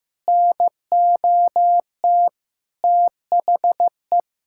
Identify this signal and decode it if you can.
Morse code, heard as a keyed tone, about 15 wpm, 700 Hz. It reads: NOT THE
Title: Sentences from Top 1000 Words - Review Part 6 - 15wpm